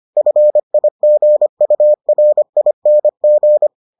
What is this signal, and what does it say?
Morse code, heard as a keyed tone, about 25 wpm, 600 Hz. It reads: FIGURING